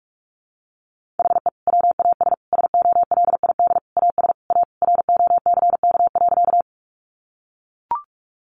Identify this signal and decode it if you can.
Morse code, heard as a keyed tone, about 44 wpm, 700 Hz. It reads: HE WAS SOLID AS A ROCK.
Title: Sentences from Top 1500 Words 45wpm